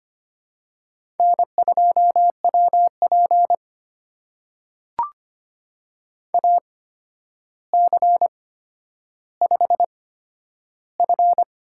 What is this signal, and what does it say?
Morse code, heard as a keyed tone, about 25 wpm, spaced as for 11 wpm, 700 Hz. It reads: N2WP AC5F
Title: USA Call Signs - 4 Characters - 8x Character Spacing - 25wpm